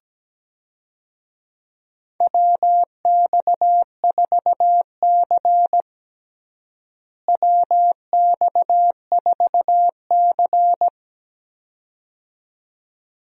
Morse code, heard as a keyed tone, about 17 wpm, 700 Hz. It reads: WX4C WX4C
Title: USA Call Signs - 4 Characters - Repeated 2x - 17wpm